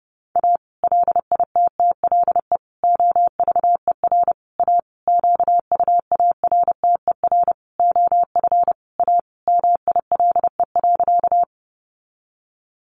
Morse code, heard as a keyed tone, about 30 wpm, 700 Hz. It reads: A LITTLE OVER A QUARTER OF A MILE.
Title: Sentences from Top 1300 Words 30wpm